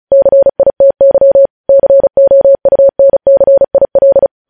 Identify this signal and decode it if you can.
Morse code, heard as a keyed tone, about 36 wpm, 550 Hz. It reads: CITY COUNCIL
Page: News Headlines in Morse Code at 35 WPM